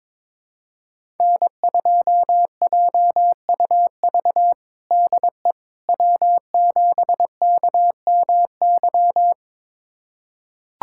Morse code, heard as a keyed tone, about 21 wpm, 700 Hz. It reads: N2JUV DE W7KMY
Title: USA Callsign DE Callsign - 5 Characters - 22wpm